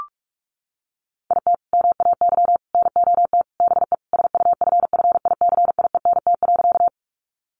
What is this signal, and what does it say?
Morse code, heard as a keyed tone, about 44 wpm, 700 Hz. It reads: IT MAY NOT BE SUFFICIENT.
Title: Sentences from Top 1400 Words 45wpm